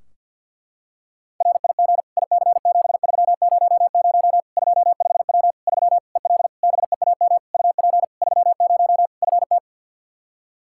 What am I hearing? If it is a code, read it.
Morse code, heard as a keyed tone, about 53 wpm, 700 Hz. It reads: RIG IC7300 25W 3 EL BEAM UP 30 FT
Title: Practice QSO - Exchange 2 - Info - 50wpm